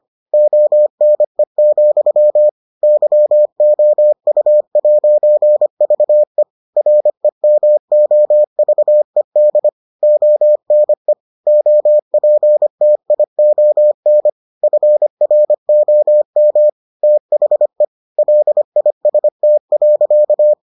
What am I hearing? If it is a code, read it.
Morse code, heard as a keyed tone, about 25 wpm, 600 Hz. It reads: ONE, YOU'VE REMOVED ONE OPTION FROM THE LIST.